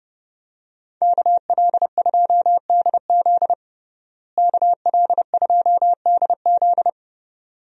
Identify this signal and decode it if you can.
Morse code, heard as a keyed tone, about 30 wpm, 700 Hz. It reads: KL2DZ KL2DZ
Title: USA Call Signs - 5 Characters - Repeated 2x - 30wpm